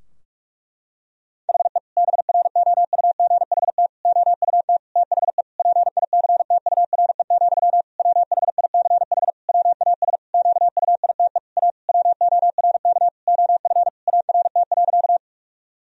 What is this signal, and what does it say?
Morse code, heard as a keyed tone, about 46 wpm, 700 Hz. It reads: HE BROUGHT OUT THE PICTURE, WHICH WAS QUITE A WORK OF ART.
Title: Sentences from Top 1000 Words - Review Part 3 - 45wpm